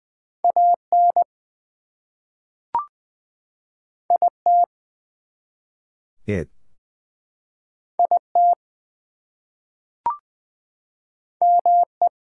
Morse code, heard as a keyed tone, about 20 wpm, 700 Hz. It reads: AN IT IT ME
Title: Two Letter Words - 20wpm